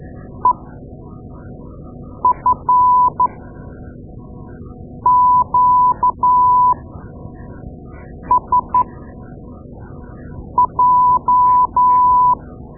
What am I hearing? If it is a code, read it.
Morse code, heard as a keyed tone, about 13 wpm, spaced as for 4 wpm, 1000 Hz. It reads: EFQSJ
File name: Morse Lessons / Morse Lessons / Tape-2A